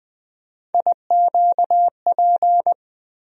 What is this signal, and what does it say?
Morse code, heard as a keyed tone, about 20 wpm, 700 Hz. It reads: IQP
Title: USA Call Sign Postfixes - 3 Letters 20wpm